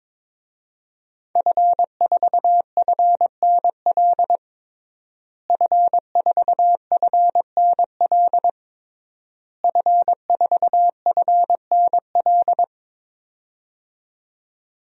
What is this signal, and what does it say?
Morse code, heard as a keyed tone, about 22 wpm, 700 Hz. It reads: F4FNL F4FNL F4FNL
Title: DXCC-100 Call Signs - 5 Characters - Repeated 3x - 22wpm